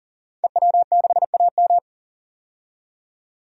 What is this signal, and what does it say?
Morse code, heard as a keyed tone, about 40 wpm, 700 Hz. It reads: EW6AM
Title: DXCC-100 Call Signs - 5 Characters - 40wpm